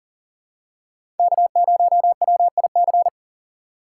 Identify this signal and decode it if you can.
Morse code, heard as a keyed tone, about 40 wpm, 700 Hz. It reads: K0WIC
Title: USA Call Signs - 5 Characters - 3x Character Spacing - 40wpm